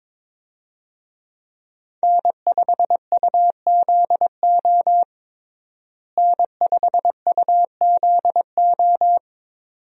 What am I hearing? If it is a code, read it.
Morse code, heard as a keyed tone, about 22 wpm, 700 Hz. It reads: N5UZO N5UZO
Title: USA Call Signs - 5 Characters - Repeated 2x - 22wpm